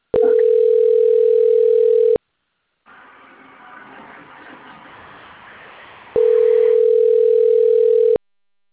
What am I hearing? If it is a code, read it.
Ringback tone (North America), 440 + 480 Hz.